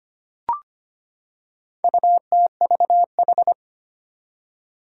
Morse code, heard as a keyed tone, about 25 wpm, 700 Hz. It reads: UTVH
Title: Four Challenging Letter-Number Combinations 25wpm